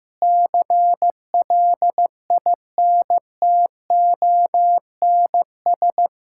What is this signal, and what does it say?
Morse code, heard as a keyed tone, about 15 wpm, 700 Hz. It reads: CLINTONS
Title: Pangram Challenge v2 - 15wpm